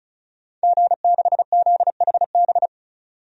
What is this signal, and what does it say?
Morse code, heard as a keyed tone, about 36 wpm, 700 Hz. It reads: G6ZHB
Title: DXCC-100 Call Signs - 5 Characters - 35wpm